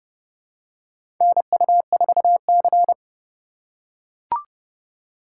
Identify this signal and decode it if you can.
Morse code, heard as a keyed tone, about 32 wpm, 700 Hz. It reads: NU4C
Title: USA Call Signs - 4 Characters - 3x Character Spacing - 30wpm